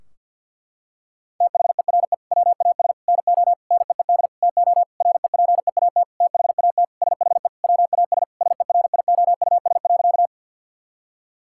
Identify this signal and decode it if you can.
Morse code, heard as a keyed tone, about 50 wpm, 700 Hz. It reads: THERE WAS NO NEED TO REPEAT THAT SHE WAS SERIOUS.
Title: Sentences from Top 1600 Words 50wpm